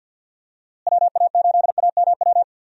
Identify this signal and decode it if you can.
Morse code, heard as a keyed tone, about 53 wpm, 700 Hz. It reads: WA8AMW
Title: USA Call Signs - 6 Characters - Repeated 3x - 50wpm